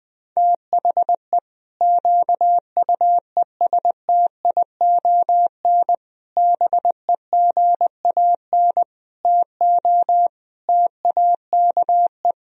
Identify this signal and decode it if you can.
Morse code, heard as a keyed tone, about 20 wpm, 700 Hz. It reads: THE QUESTION BEGAN TO TAKE
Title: Sentences from Top 1000 Words - Review Part 3 - 20wpm